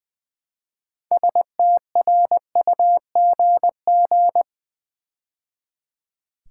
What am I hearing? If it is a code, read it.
Morse code, heard as a keyed tone, about 20 wpm, 700 Hz. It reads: STRUGG
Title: Build A Word - 2 Letters At A Time - 8-Letter Words - 20wpm